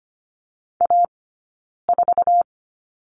Morse code, heard as a keyed tone, about 25 wpm, 700 Hz. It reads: A 4
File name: course / 25:10wpm / 092 Lesson 16d - Letter c - Two Characters 25wpm